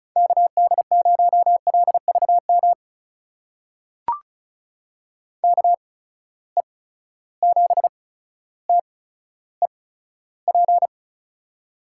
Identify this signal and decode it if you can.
Morse code, heard as a keyed tone, about 36 wpm, spaced as for 15 wpm, 700 Hz. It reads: KD0LVM KE7TEP
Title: USA Call Signs - 6 Characters - 8x Character Spacing - 35wpm